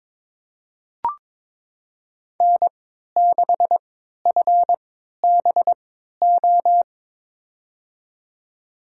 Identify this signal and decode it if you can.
Morse code, heard as a keyed tone, about 22 wpm, spaced as for 13 wpm, 700 Hz. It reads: N6FBO
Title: USA Call Signs - 5 Characters - 3x Character Spacing - 22wpm